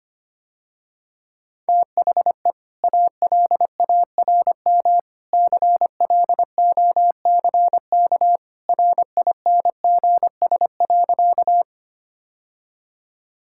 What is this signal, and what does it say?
Morse code, heard as a keyed tone, about 25 wpm, 700 Hz. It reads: THE ALARM CLOCK RINGS.
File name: Easy Sentences with Hints - 25wpm